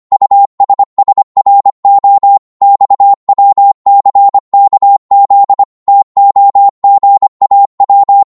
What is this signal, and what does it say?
Morse code, heard as a keyed tone, about 25 wpm, 800 Hz. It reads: USSRO XWCKZ TOGAW